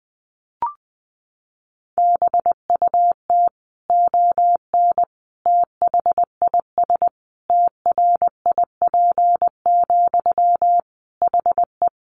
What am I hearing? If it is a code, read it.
Morse code, heard as a keyed tone, about 20 wpm, 700 Hz. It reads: BUT ON THIS TRIP, HE
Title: Sentences from Top 1400 Words 20wpm